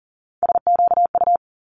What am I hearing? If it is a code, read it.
Morse code, heard as a keyed tone, about 40 wpm, 700 Hz. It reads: SQU
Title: Multi-letter Phonemes Quick Fire 40wpm